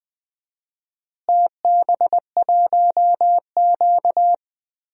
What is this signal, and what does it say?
Morse code, heard as a keyed tone, about 20 wpm, 700 Hz. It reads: TB1Q
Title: DXCC-100 Call Signs - 4 Characters - Repeated 3x - 20wpm